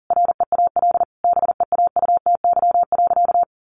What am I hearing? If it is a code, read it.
Morse code, heard as a keyed tone, about 40 wpm, 700 Hz. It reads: REAL BEAUTY.
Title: Sentences from Top 1100 Words 40wpm